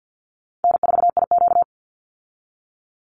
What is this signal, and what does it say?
Morse code, heard as a keyed tone, about 50 wpm, 700 Hz. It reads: N4IQ
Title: Call Signs - 50wpm